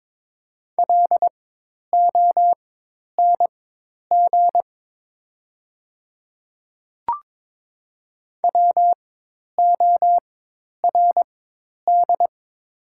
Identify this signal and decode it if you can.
Morse code, heard as a keyed tone, about 22 wpm, spaced as for 10 wpm, 700 Hz. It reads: LONG WORD
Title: 4x Character Spacing - Top 200 Words - 22wpm